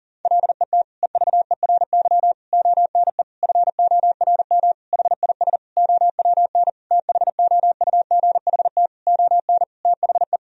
Morse code, heard as a keyed tone, about 37 wpm, 700 Hz. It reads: LET EVERY ONE FORM HIS OWN THOUGHT ON THE
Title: Sentences from Top 1000 Words - Review Part 1 - 40wpm